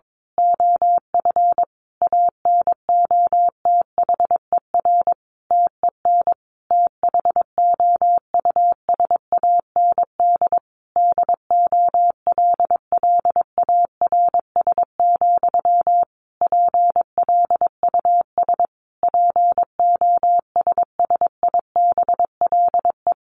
Morse code, heard as a keyed tone, about 22 wpm, 700 Hz. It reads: OF ANOTHER TEN THOUSAND DOLLARS, PLUS POSSIBLE
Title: A Sound Of Thunder 22wpm